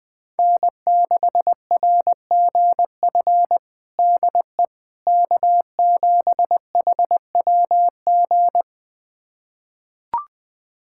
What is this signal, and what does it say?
Morse code, heard as a keyed tone, about 19 wpm, 700 Hz. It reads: N6RGF DE K7HWG
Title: USA Callsign DE Callsign - 5 Characters - 20wpm